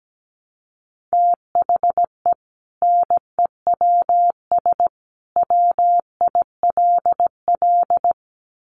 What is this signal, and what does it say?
Morse code, heard as a keyed tone, about 17 wpm, 700 Hz. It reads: THE NEWS WILL
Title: Sentences from Top 1200 Words 17wpm